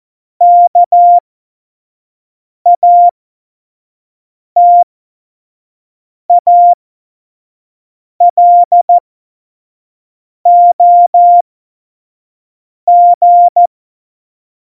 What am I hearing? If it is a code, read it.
Morse code, heard as a keyed tone, about 14 wpm, spaced as for 5 wpm, 700 Hz. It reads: KATALOG
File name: Morsekurs / Witze / Witz 23. Juli 2019, 25 F70